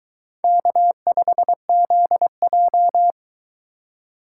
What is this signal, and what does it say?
Morse code, heard as a keyed tone, about 21 wpm, 700 Hz. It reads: K5ZJ